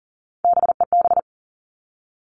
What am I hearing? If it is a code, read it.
Morse code, heard as a keyed tone, about 40 wpm, 700 Hz. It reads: BEB
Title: All Cut Numbers - 3-Digit 40wpm